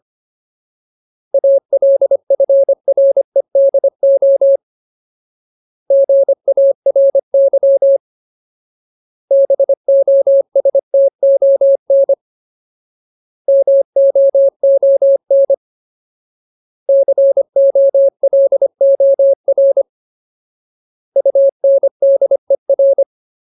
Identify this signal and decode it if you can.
Morse code, heard as a keyed tone, about 25 wpm, 550 Hz. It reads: ALFREDO GARY BOSTON MOON COLOR UNDER
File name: Ultimate Word List / 25WPM Ultimate